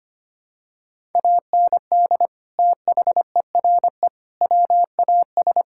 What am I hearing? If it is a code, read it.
Morse code, heard as a keyed tone, about 25 wpm, 700 Hz. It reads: AND THERE WAS
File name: practice / 25wpm / Sentences with 3-Word Phrases - 25wpm